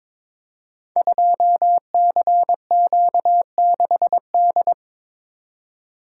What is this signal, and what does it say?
Morse code, heard as a keyed tone, about 22 wpm, 700 Hz. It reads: 2CQ6D
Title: Code Groups - 5 Letter:Numbers - 22wpm